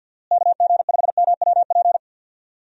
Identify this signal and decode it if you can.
Morse code, heard as a keyed tone, about 53 wpm, 700 Hz. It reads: KG5MWP
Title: USA Call Signs - 6 Characters - Repeated 3x - 50wpm